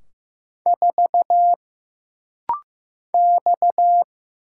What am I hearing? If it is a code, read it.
Morse code, heard as a keyed tone, about 15 wpm, 700 Hz. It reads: 4 X
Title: Single Letter-Number Alt - Rapid-Fire 15wpm